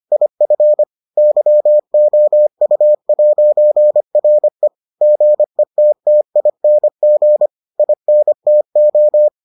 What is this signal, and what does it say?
Morse code, heard as a keyed tone, about 25 wpm, 600 Hz. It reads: IF YOU'RE GETTING INTO